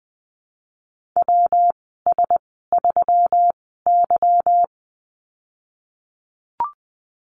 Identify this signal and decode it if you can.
Morse code, heard as a keyed tone, about 20 wpm, spaced as for 15 wpm, 700 Hz. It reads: WS3Y
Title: 161 Lesson 26g - Letter g - Call Signs 20wpm